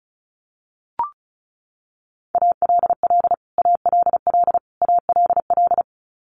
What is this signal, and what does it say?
Morse code, heard as a keyed tone, about 34 wpm, 700 Hz. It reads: ALL ALL ALL
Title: Top 200 Words Repeated 35wpm